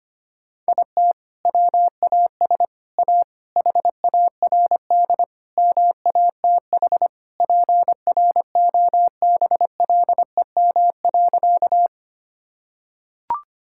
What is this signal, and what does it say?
Morse code, heard as a keyed tone, about 25 wpm, 700 Hz. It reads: IT WAS A HARD MATH PROBLEM.
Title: Sentences from Top 1000 Words - Review Part 6 - 25wpm